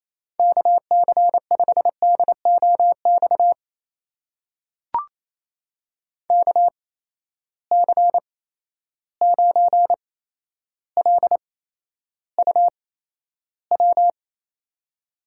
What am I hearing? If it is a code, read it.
Morse code, heard as a keyed tone, about 28 wpm, spaced as for 12 wpm, 700 Hz. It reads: KC5DOX KC9LUW
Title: USA Call Signs - 6 Characters - 8x Character Spacing - 28wpm